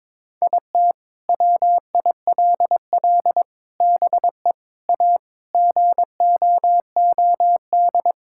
Morse code, heard as a keyed tone, about 22 wpm, 700 Hz. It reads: IT WILL BE A GOOD
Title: Sentences from Top 1000 Words - Review Part 6 - 22wpm